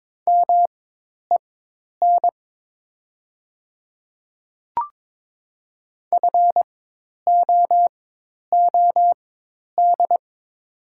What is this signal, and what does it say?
Morse code, heard as a keyed tone, about 23 wpm, spaced as for 10 wpm, 700 Hz. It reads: MEN FOOD
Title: 4x Character Spacing - Top 300 Words - 22wpm